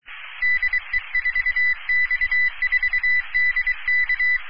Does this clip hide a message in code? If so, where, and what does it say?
Yes: Morse code from 0.4 to 4.4 s, heard as a keyed tone, about 23 wpm, 2000 Hz. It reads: DE4X4DK